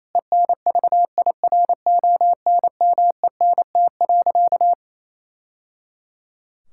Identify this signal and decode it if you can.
Morse code, heard as a keyed tone, about 27 wpm, 700 Hz. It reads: ENVIRONMENT.
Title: Sentences from Top 1000 Words - Review Part 5 - 28wpm